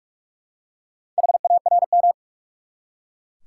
Morse code, heard as a keyed tone, about 42 wpm, 700 Hz. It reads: HARM
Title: Four Letter Words - 45wpm